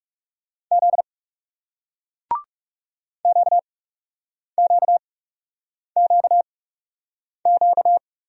Morse code, heard as a keyed tone, about 41 wpm, spaced as for 8 wpm, 700 Hz. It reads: Z QQQQ